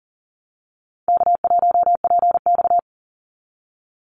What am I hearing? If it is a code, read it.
Morse code, heard as a keyed tone, about 40 wpm, 700 Hz. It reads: K1PX
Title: Easy Call Signs Repeated 3x - 40wpm